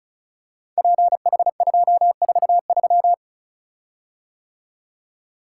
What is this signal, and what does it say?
Morse code, heard as a keyed tone, about 36 wpm, 700 Hz. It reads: PH243
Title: Code Groups - 5 Letter:Numbers - 35wpm